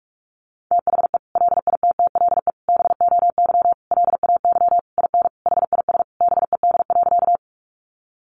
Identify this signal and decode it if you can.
Morse code, heard as a keyed tone, about 44 wpm, 700 Hz. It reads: THE LITTLE BOY LAY IN HIS BED.
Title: Sentences from Top 500 Words 45wpm